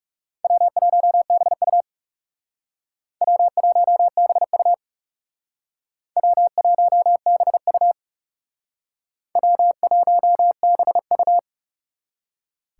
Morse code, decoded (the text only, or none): W1BU W1BU W1BU W1BU